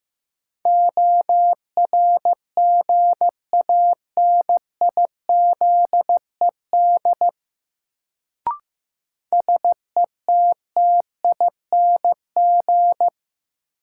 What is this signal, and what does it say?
Morse code, heard as a keyed tone, about 15 wpm, 700 Hz. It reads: ORGANIZED SETTING
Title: Sentences from Top 1300 Words - New Word Drill - 15wpm